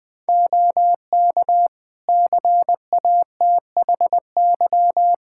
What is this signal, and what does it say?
Morse code, heard as a keyed tone, about 21 wpm, 700 Hz. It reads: OK CATHY
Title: K1USN SST QSO Practice 20wpm